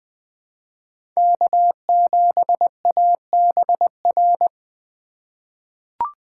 Morse code, heard as a keyed tone, about 20 wpm, 700 Hz. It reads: K7ABR
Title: USA Call Signs - 5 Characters - 4x Character Spacing - 20wpm